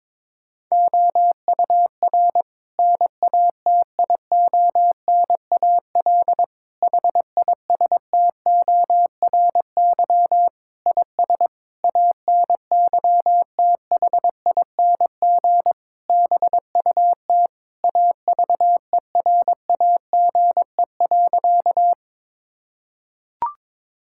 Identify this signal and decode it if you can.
Morse code, heard as a keyed tone, about 21 wpm, 700 Hz. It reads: OUR NATIONAL HISTORY IS ANYTHING BUT AVERAGE.
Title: Sentences from Top 1000 Words - Review Part 6 - 22wpm